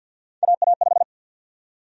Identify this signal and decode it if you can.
Morse code, heard as a keyed tone, about 52 wpm, 700 Hz. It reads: AA5
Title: USA Call Sign Prefixes - Repeated 2x - 50wpm